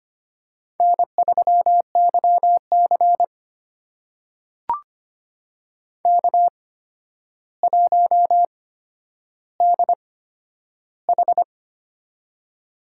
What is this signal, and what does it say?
Morse code, heard as a keyed tone, about 25 wpm, spaced as for 11 wpm, 700 Hz. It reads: N3YC K1DH
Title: USA Call Signs - 4 Characters - 8x Character Spacing - 25wpm